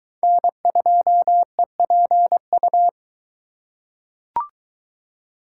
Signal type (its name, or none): Morse code